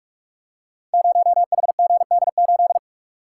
Morse code, heard as a keyed tone, about 42 wpm, 700 Hz. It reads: 0HGD8
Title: Code Groups - 5 Letter:Numbers - 45wpm